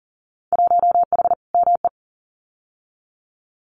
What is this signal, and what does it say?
Morse code, heard as a keyed tone, about 40 wpm, 700 Hz. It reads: 1H ME
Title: Winter Field Day Exchanges Encore 40wpm